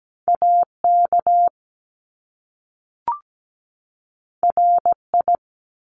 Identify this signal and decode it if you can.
Morse code, heard as a keyed tone, about 17 wpm, 700 Hz. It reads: AK RI
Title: ARRL Sections 17wpm